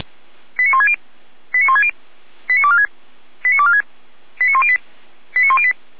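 Five-tone selective call (ZVEI), heard as quick runs of five tones, tones of about 70 ms. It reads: 89160 89160 89246 89246 89108 89108
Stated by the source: ZVEI five-tone selective call received by SDR.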